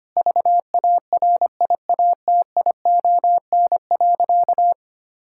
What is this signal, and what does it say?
Morse code, heard as a keyed tone, about 25 wpm, 700 Hz. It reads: VARIATION.
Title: Sentences from Top 1600 Words 25wpm